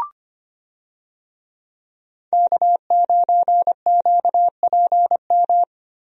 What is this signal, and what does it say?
Morse code, heard as a keyed tone, about 25 wpm, 700 Hz. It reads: K9QPM
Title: USA Call Signs - 5 Characters - Repeated 2x - 25wpm